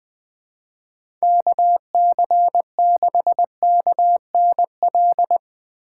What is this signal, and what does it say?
Morse code, heard as a keyed tone, about 20 wpm, 700 Hz. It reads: KC6KNL
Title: USA Call Signs - 6 Characters - 3x Character Spacing - 20wpm